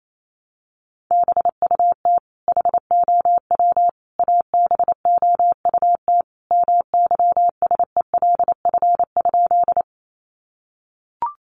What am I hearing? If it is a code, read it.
Morse code, heard as a keyed tone, about 28 wpm, 700 Hz. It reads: BUT HOW ABOUT MYSELF?
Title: Sentences from Top 1300 Words 28wpm